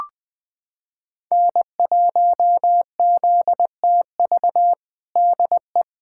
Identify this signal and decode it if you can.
Morse code, heard as a keyed tone, about 20 wpm, 700 Hz. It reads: N1ZTV DE
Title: USA Callsign DE Callsign 20wpm